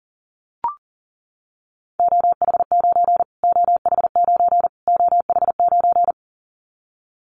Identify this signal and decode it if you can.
Morse code, heard as a keyed tone, about 40 wpm, 700 Hz. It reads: OH9 OH9 OH9
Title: Callsign Prefixes for Top 25 DXCC - Repeated 3x - 40wpm